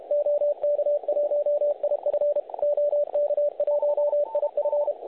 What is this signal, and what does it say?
Morse code, heard as a keyed tone, about 32 wpm, 590 Hz. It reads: OK1IF OK1IF
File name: truSDX / websdr recording 3541.9kHz